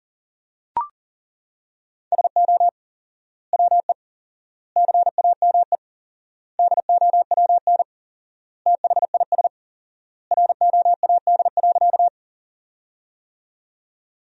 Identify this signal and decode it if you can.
Morse code, heard as a keyed tone, about 40 wpm, 700 Hz. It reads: SO WE CAME DOWN THIS ROAD.
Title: 4x Word-Spacing - Sentences from Top 400 Words - 40wpm